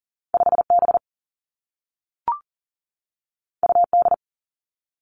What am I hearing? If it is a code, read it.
Morse code, heard as a keyed tone, about 40 wpm, 700 Hz. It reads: 5B UD
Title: Two Challenging Letter-Number Combinations 40wpm